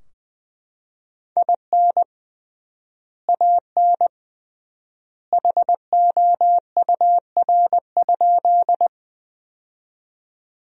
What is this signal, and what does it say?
Morse code, heard as a keyed tone, about 20 wpm, 700 Hz. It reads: IN AN HOUR?
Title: 3x Word-Spacing - Sentences from Top 400 Words - 20wpm